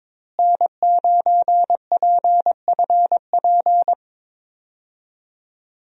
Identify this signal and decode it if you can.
Morse code, heard as a keyed tone, about 22 wpm, 700 Hz. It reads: N9PFP
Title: USA Call Signs - 5 Characters - Repeated 2x - 22wpm